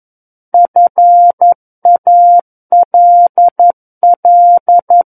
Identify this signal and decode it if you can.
Morse code, heard as a keyed tone, about 11 wpm, 700 Hz. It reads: FALL
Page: SKCC - War of the Worlds - 02-Chapter 2 - 11 WPM